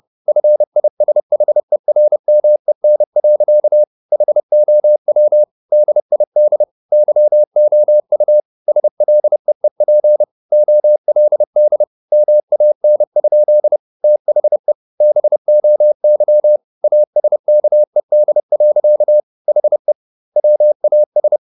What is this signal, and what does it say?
Morse code, heard as a keyed tone, about 30 wpm, 600 Hz. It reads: FISHERMEN. HOW DID YOU SLEEP OLD MAN? THE BOY ASKED. HE WAS
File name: the old man and the sea / the old man and the sea 1 30wpm